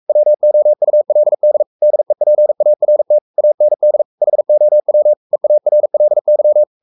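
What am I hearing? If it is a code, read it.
Morse code, heard as a keyed tone, about 44 wpm, 600 Hz. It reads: WOULD DEPART AND HOW EARLY
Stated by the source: Morse code at 43 wpm sending Ken Follett's novel Never in English, from CW med Gnister.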